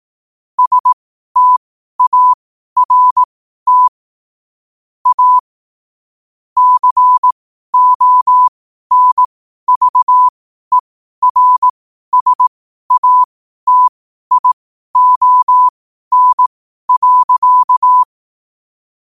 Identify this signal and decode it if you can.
Morse code, heard as a keyed tone, about 18 wpm, spaced as for 12 wpm, 1000 Hz. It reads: START A CONVERSATION.
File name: qotdfiles / 12wpm